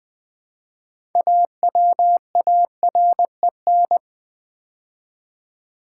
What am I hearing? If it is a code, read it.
Morse code, heard as a keyed tone, about 20 wpm, 700 Hz. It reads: AWAREN
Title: Build A Word - 3 Letters At A Time - 9-Letter Words - 20wpm